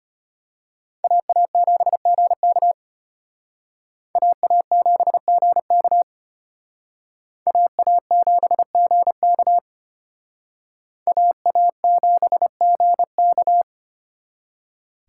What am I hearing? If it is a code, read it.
Morse code, heard as a keyed tone, about 33 wpm, 700 Hz. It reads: AA7GK AA7GK AA7GK AA7GK